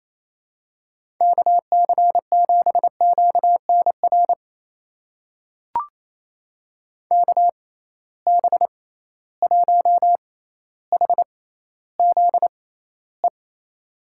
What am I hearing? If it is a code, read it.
Morse code, heard as a keyed tone, about 28 wpm, spaced as for 14 wpm, 700 Hz. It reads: KC7QNR KB1HZE
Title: USA Call Signs - 6 Characters - 6x Character Spacing - 28wpm